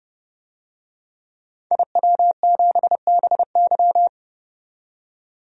Morse code, heard as a keyed tone, about 30 wpm, 700 Hz. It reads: IW7BY
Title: DXCC-100 Call Signs - 5 Characters - 30wpm